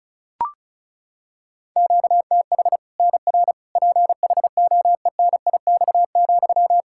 Morse code, heard as a keyed tone, about 34 wpm, 700 Hz. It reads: QTH NR PHOENIX,
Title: Practice QSO - Exchange 1 - Beginning - 35wpm